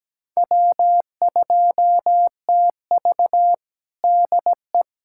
Morse code, heard as a keyed tone, about 17 wpm, 700 Hz. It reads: W2TV DE
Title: USA Callsign DE Callsign - 4 Characters - 17wpm